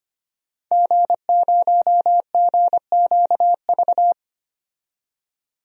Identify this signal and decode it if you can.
Morse code, heard as a keyed tone, about 25 wpm, 700 Hz. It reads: G0GQV
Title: DXCC-100 Call Signs - 5 Characters - 25wpm